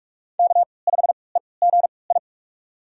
Morse code, heard as a keyed tone, about 47 wpm, spaced as for 26 wpm, 700 Hz. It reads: K5EGI
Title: USA Call Signs - 5 Characters - 3x Character Spacing - 45wpm